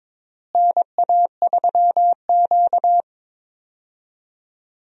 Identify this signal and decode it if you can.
Morse code, heard as a keyed tone, about 22 wpm, 700 Hz. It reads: NA3Q
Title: USA Call Signs - 4 Characters - Repeated 3x - 22wpm